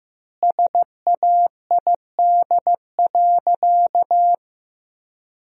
Morse code, heard as a keyed tone, about 15 wpm, 700 Hz. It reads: SAID.
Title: 2x Word-Spacing - Sentences from Top 400 Words - 15wpm